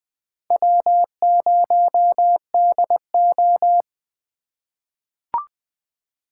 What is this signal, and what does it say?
Morse code, heard as a keyed tone, about 20 wpm, 700 Hz. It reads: W0DO
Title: USA Call Signs - 4 Characters - 4x Character Spacing - 20wpm